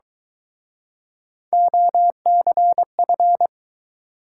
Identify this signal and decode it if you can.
Morse code, heard as a keyed tone, about 23 wpm, 700 Hz. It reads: OCF